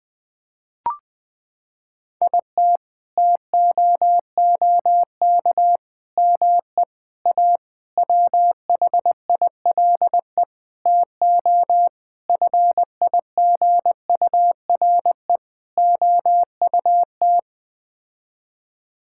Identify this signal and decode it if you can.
Morse code, heard as a keyed tone, about 20 wpm, 700 Hz. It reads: IT TOOK ME A WHILE TO FIGURE OUT
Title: Sets of 8 Words - 20wpm